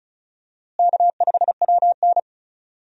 Morse code, heard as a keyed tone, about 36 wpm, 700 Hz. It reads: K5WN
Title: USA Call Signs - 4 Characters - Repeated 3x - 35wpm